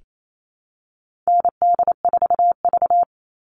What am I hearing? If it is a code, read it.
Morse code, heard as a keyed tone, about 28 wpm, 700 Hz. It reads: ND4V
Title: Call Signs Repeated 2x - 28wpm